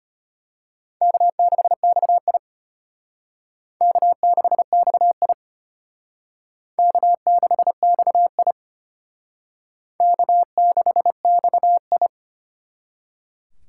Morse code, heard as a keyed tone, about 32 wpm, 700 Hz. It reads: K6XI K6XI K6XI K6XI